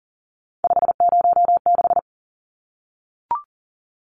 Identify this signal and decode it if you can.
Morse code, heard as a keyed tone, about 40 wpm, 700 Hz. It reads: 506
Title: Numbers - 3-Digit 40wpm